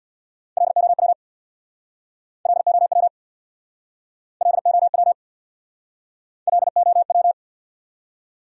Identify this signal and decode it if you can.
Morse code, heard as a keyed tone, about 69 wpm, 700 Hz. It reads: LOW LOW LOW LOW